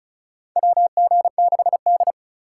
Morse code, heard as a keyed tone, about 34 wpm, 700 Hz. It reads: WG6D
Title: USA Call Signs - 4 Characters - 4x Character Spacing - 35wpm